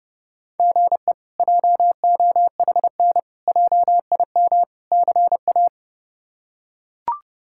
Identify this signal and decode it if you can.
Morse code, heard as a keyed tone, about 30 wpm, 700 Hz. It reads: GE JOHN JIM CA
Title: K1USN SST QSO Practice - 30wpm